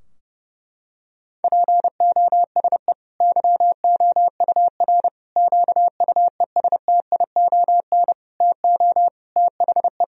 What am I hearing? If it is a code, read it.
Morse code, heard as a keyed tone, about 30 wpm, 700 Hz. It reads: POSE YOUR QUESTION TO THE
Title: Sentences from Top 1000 Words - Review Part 2 - 30wpm